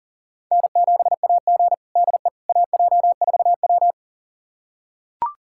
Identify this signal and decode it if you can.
Morse code, heard as a keyed tone, about 40 wpm, 700 Hz. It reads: N7AG DE AJ4W
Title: USA Callsign DE Callsign - 4 Characters - 40wpm